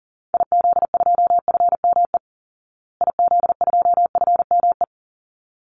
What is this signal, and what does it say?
Morse code, heard as a keyed tone, about 40 wpm, 700 Hz. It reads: IZ2FME IZ2FME
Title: Call Signs Repeated 3x - 40wpm